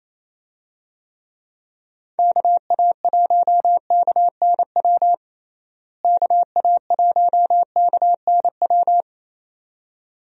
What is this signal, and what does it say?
Morse code, heard as a keyed tone, about 28 wpm, 700 Hz. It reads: KA1KNW KA1KNW
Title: USA Call Signs - 6 Characters - Repeated 2x - 28wpm